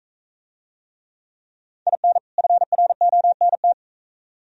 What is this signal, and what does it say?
Morse code, heard as a keyed tone, about 44 wpm, 700 Hz. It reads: IN FRONT